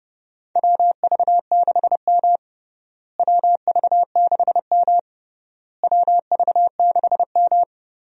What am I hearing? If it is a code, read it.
Morse code, heard as a keyed tone, about 30 wpm, 700 Hz. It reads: WV6M WV6M WV6M
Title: USA Call Signs - 4 Characters - Repeated 3x - 30wpm